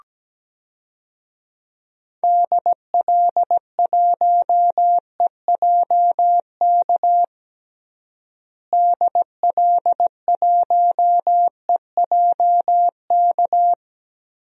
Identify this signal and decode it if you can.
Morse code, heard as a keyed tone, about 17 wpm, 700 Hz. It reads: DL1EJK DL1EJK
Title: DXCC-100 Call Signs - 6 Characters - Repeated 3x - 17wpm